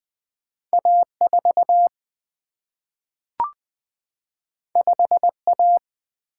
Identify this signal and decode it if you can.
Morse code, heard as a keyed tone, about 20 wpm, 700 Hz. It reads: A4 5A
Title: Single-Letter Single-Number 20wpm